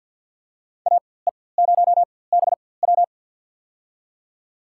Morse code, heard as a keyed tone, about 48 wpm, spaced as for 23 wpm, 700 Hz. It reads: AE0BW
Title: USA Call Signs - 5 Characters - 4x Character Spacing - 50wpm